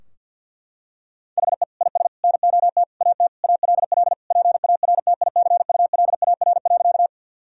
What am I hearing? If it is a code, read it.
Morse code, heard as a keyed tone, about 52 wpm, 700 Hz. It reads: HE IS NOT AT ALL PARTICULAR.
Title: Sentences from Top 1000 Words - Review Part 4 - 50wpm